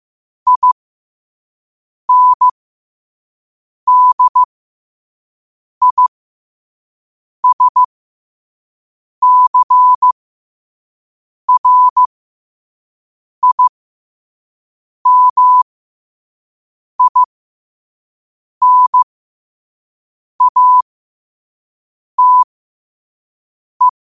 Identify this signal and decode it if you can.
Morse code, heard as a keyed tone, about 15 wpm, spaced as for 5 wpm, 1000 Hz. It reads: INDISCRIMINATE
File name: qotdfiles / 5wpm